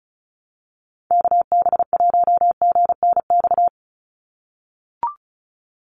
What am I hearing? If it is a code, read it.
Morse code, heard as a keyed tone, about 34 wpm, 700 Hz. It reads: KB1GNX
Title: USA Call Signs 35wpm